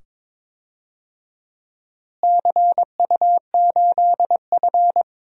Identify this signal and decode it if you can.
Morse code, heard as a keyed tone, about 22 wpm, 700 Hz. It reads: CU8F
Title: DXCC-100 Call Signs - 4 Characters - Repeated 2x - 22wpm